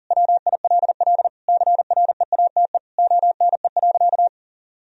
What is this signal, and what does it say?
Morse code, heard as a keyed tone, about 40 wpm, 700 Hz. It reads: WILL CREATE ONE.
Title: Sentences from Top 1000 Words - Review Part 3 - 40wpm